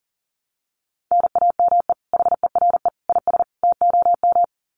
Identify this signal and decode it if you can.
Morse code, heard as a keyed tone, about 40 wpm, 700 Hz. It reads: NAME HERE IS TOM
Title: Common QSO Phrases 40wpm